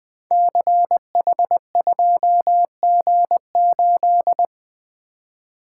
Morse code, heard as a keyed tone, about 21 wpm, 700 Hz. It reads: CH2G8
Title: Code Groups - 5 Letter:Numbers - 20wpm